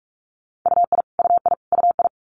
Morse code, heard as a keyed tone, about 46 wpm, 700 Hz. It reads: UI UI UI
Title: Multi-letter Phonemes Repeated 3x 45wpm